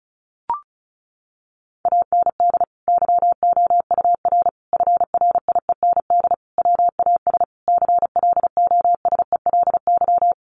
Morse code, heard as a keyed tone, about 34 wpm, 700 Hz. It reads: AND YOUR FRIEND WAS CLOSELY
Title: Sentences from Top 1400 Words 35wpm